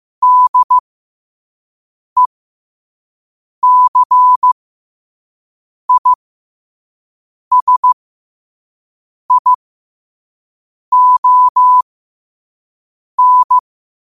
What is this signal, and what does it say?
Morse code, heard as a keyed tone, about 15 wpm, spaced as for 5 wpm, 1000 Hz. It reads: DECISION
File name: qotdfiles / 5wpm